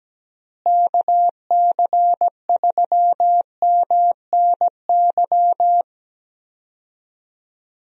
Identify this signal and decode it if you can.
Morse code, heard as a keyed tone, about 17 wpm, 700 Hz. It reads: KC3MNY
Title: USA Call Signs - 6 Characters - Repeated 2x - 17wpm